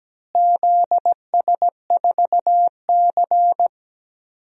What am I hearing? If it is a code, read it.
Morse code, heard as a keyed tone, about 17 wpm, 700 Hz. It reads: ZS4C
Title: DXCC-100 Call Signs - 4 Characters - Repeated 3x - 17wpm